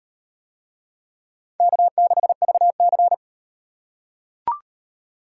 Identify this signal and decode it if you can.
Morse code, heard as a keyed tone, about 34 wpm, 700 Hz. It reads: K6VC